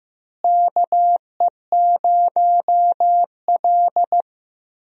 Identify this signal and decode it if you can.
Morse code, heard as a keyed tone, about 15 wpm, 700 Hz. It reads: KE0L
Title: USA Call Signs - 4 Characters - Repeated 3x - 15wpm